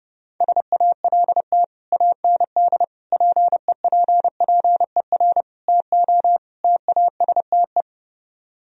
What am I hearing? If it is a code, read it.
Morse code, heard as a keyed tone, about 30 wpm, 700 Hz. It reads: SALT AND PEPPER TO TASTE